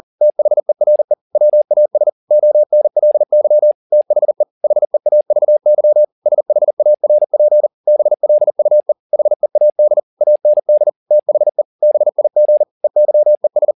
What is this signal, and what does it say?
Morse code, heard as a keyed tone, about 40 wpm, 600 Hz. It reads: THERE WAS ONLY THE HEAVY SHARP BLUE HEAD AND THE BIG EYES
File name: the old man and the sea 2 40wpm